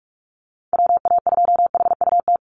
Morse code, heard as a keyed tone, about 48 wpm, 700 Hz. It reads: WA2HUT
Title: USA Call Signs Repeated 2x 45wpm